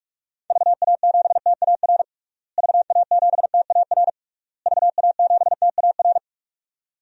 Morse code, heard as a keyed tone, about 46 wpm, 700 Hz. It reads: VA7TAR VA7TAR VA7TAR
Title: DXCC-100 Call Signs - 6 Characters - Repeated 3x - 45wpm